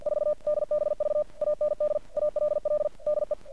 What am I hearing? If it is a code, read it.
Morse code, heard as a keyed tone, about 50 wpm, 600 Hz. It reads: 4 DBV AND ALL DE
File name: archived / CRSnet